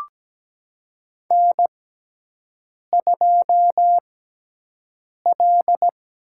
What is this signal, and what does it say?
Morse code, heard as a keyed tone, about 17 wpm, spaced as for 6 wpm, 700 Hz. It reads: N2L
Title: USA Call Signs - 4 Characters - 6x Character Spacing - 17wpm